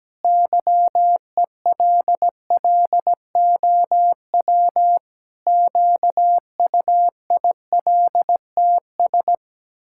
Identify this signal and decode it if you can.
Morse code, heard as a keyed tone, about 17 wpm, 700 Hz. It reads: YELLOW QUILTS
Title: Pangram Challenge v2 - 17wpm